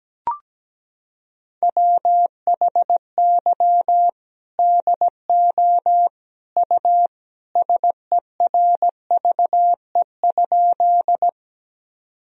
Morse code, heard as a keyed tone, about 17 wpm, 700 Hz. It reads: WHY DO U SERVE?
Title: Sentences from Top 400 Words and Abbreviations 17wpm